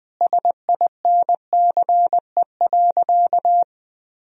Morse code, heard as a keyed tone, about 20 wpm, 700 Hz. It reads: SINCE.
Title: Sentences from Top 1000 Words - Review Part 3 - 20wpm